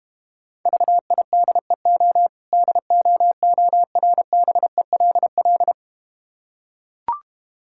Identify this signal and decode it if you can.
Morse code, heard as a keyed tone, about 31 wpm, 700 Hz. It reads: VIDEO DOORBELL
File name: practice / 25wpm / Difficult Words - Speed-Racing with Hints - 25wpm